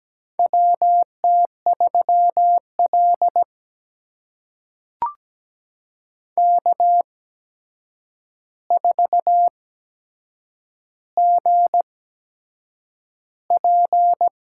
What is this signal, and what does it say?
Morse code, heard as a keyed tone, about 17 wpm, spaced as for 7 wpm, 700 Hz. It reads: WT3L K4GP
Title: USA Call Signs - 8x Character Spacing - 17wpm